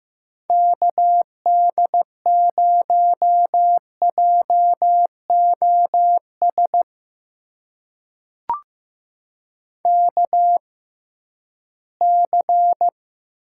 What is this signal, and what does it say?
Morse code, heard as a keyed tone, about 15 wpm, spaced as for 11 wpm, 700 Hz. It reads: KD0JOS KC
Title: USA Call Signs - 6 Characters - 6x Character Spacing - 15wpm